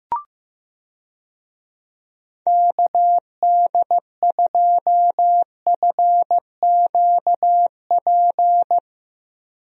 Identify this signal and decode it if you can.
Morse code, heard as a keyed tone, about 15 wpm, 700 Hz. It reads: KD2FQP
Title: USA Call Signs - 6 Characters - Repeated 3x - 15wpm